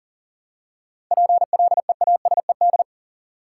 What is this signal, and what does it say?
Morse code, heard as a keyed tone, about 40 wpm, 700 Hz. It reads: PLEASED
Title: Seven Letter Words - 40wpm